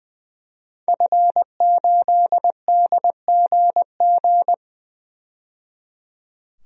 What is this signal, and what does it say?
Morse code, heard as a keyed tone, about 20 wpm, 700 Hz. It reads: F8DGG
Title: DXCC-100 Call Signs - 5 Characters - 20wpm